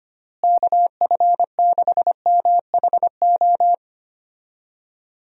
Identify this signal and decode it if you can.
Morse code, heard as a keyed tone, about 25 wpm, 700 Hz. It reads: KF6MHO
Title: USA Call Signs - 6 Characters - Repeated 2x - 25wpm